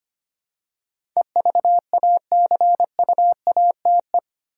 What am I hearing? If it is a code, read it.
Morse code, heard as a keyed tone, about 25 wpm, 700 Hz. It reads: EVACUATE
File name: 8 Letter Words - First 3 Letters Given - 25wpm